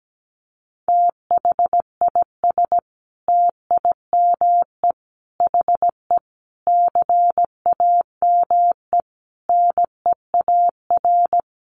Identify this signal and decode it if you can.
Morse code, heard as a keyed tone, about 17 wpm, 700 Hz. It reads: THIS TIME HE CAME NEAR
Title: Sentences from Top 1200 Words 17wpm